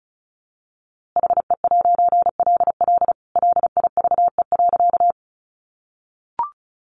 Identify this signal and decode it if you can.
Morse code, heard as a keyed tone, about 34 wpm, 700 Hz. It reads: HE'LL LIVE.
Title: A Sound Of Thunder 35wpm